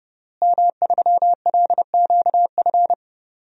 Morse code, heard as a keyed tone, about 32 wpm, 700 Hz. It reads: M3LQF
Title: DXCC-100 Call Signs - 5 Characters - Repeated 2x - 30wpm